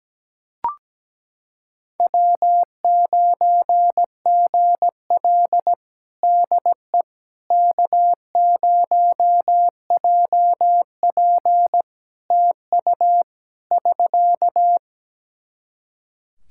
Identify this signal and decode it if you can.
Morse code, heard as a keyed tone, about 17 wpm, 700 Hz. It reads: W9GL DE K0JP TU <SK>
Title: Practice QSO - Exchange 4 - Ending - 17wpm